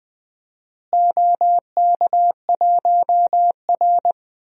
Morse code, heard as a keyed tone, about 20 wpm, 700 Hz. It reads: OK1R
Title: DXCC-100 Call Signs - 4 Characters - Repeated 2x - 20wpm